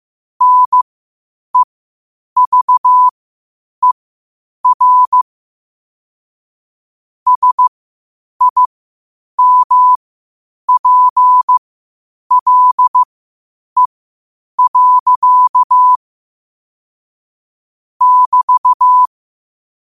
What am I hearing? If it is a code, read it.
Morse code, heard as a keyed tone, about 15 wpm, spaced as for 8 wpm, 1000 Hz. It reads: NEVER SIMPLE. <BT>
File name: qotdfiles / 8wpm